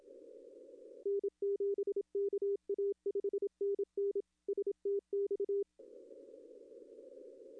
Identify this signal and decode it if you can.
Morse code, heard as a keyed tone, about 27 wpm, 400 Hz. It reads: N7KA 5NN STX
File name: contests / arrl-160-k5tr-op-w5tn -0800